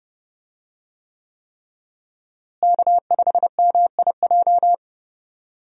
Morse code, heard as a keyed tone, about 31 wpm, 700 Hz. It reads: K5MIJ
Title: USA Call Signs - 5 Characters - Repeated 3x - 30wpm